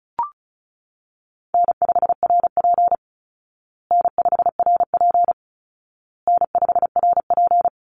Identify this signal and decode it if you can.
Morse code, heard as a keyed tone, about 34 wpm, 700 Hz. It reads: N5RP N5RP N5RP
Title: Easy Call Signs Repeated 3x - 35wpm